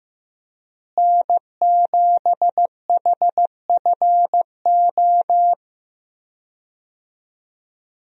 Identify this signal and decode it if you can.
Morse code, heard as a keyed tone, about 15 wpm, 700 Hz. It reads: N7HFO
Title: USA Call Signs - 5 Characters - Repeated 3x - 15wpm